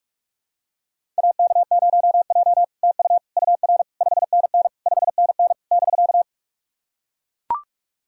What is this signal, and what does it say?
Morse code, heard as a keyed tone, about 44 wpm, 700 Hz. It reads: AK0J TU UR 5NN 5NN <BK>
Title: SOTA QSO Practice 45wpm